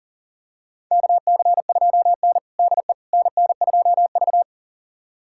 Morse code, heard as a keyed tone, about 40 wpm, 700 Hz. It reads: KC2N DE NN2V
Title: USA Callsign DE Callsign - 4 Characters - 40wpm